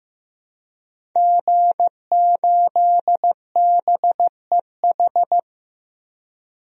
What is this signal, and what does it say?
Morse code, heard as a keyed tone, about 15 wpm, 700 Hz. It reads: G8BEH
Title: DXCC-100 Call Signs - 5 Characters - 15wpm